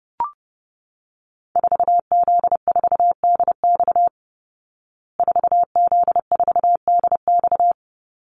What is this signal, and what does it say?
Morse code, heard as a keyed tone, about 30 wpm, 700 Hz. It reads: 4Z4DX 4Z4DX
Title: Call Signs Repeated 2x - 30wpm